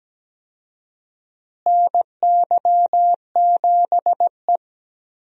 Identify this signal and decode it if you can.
Morse code, heard as a keyed tone, about 17 wpm, 700 Hz. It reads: NY7E
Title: USA Call Signs - 4 Characters - Repeated 2x - 17wpm